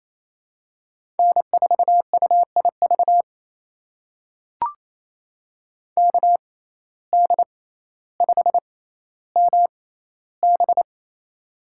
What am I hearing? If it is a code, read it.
Morse code, heard as a keyed tone, about 28 wpm, spaced as for 14 wpm, 700 Hz. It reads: N4UIV KD5MB
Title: USA Call Signs - 5 Characters - 6x Character Spacing - 28wpm